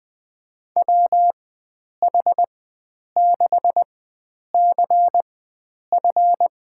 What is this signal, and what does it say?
Morse code, heard as a keyed tone, about 20 wpm, spaced as for 9 wpm, 700 Hz. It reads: WH6CF
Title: USA Call Signs - 4x Character Spacing - 20wpm